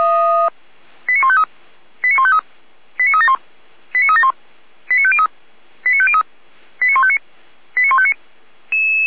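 Five-tone selective call (ZVEI), heard as quick runs of five tones, tones of about 70 ms. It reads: 89153 89153 89371 89371 89503 89503 89159 89159
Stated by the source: ZVEI five-tone selective call received by SDR.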